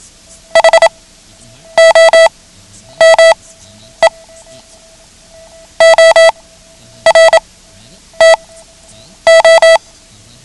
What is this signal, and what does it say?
Morse code, heard as a keyed tone, about 27 wpm, spaced as for 9 wpm, 660 Hz. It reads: HOME OR TO
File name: ham radio / CodeQuick / Code Quick 09